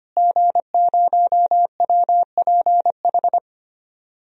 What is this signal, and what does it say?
Morse code, heard as a keyed tone, about 25 wpm, 700 Hz. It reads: G0WPH
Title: DXCC-100 Call Signs - 5 Characters - Repeated 3x - 25wpm